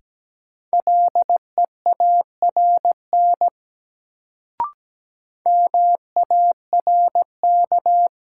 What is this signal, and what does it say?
Morse code, heard as a keyed tone, about 17 wpm, 700 Hz. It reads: LEARN MARK
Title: Sentences from Top 300 Words - New Word Drill - 17wpm